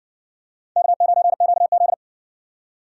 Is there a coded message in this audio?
Yes: Morse code from 0.8 to 1.9 s, heard as a keyed tone, about 60 wpm, 700 Hz. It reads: K9QZ